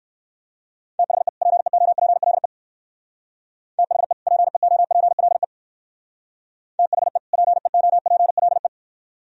Morse code, heard as a keyed tone, about 59 wpm, 700 Hz. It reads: THE PEOPLE THE PEOPLE THE PEOPLE